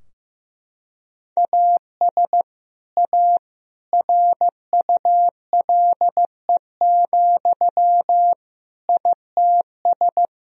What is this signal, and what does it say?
Morse code, heard as a keyed tone, about 15 wpm, 700 Hz. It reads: AS A RULE, ITS
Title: Sentences from Top 1000 Words - Review Part 2 - 15wpm